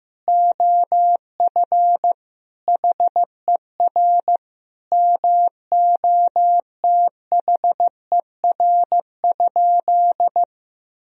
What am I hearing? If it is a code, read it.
Morse code, heard as a keyed tone, about 15 wpm, 700 Hz. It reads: OF HER MOTHER?
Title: Sentences from Top 1000 Words - Review Part 2 - 15wpm